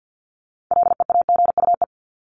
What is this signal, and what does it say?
Morse code, heard as a keyed tone, about 50 wpm, 700 Hz. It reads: LEAGUE
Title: Top 1751-2000 Words 50wpm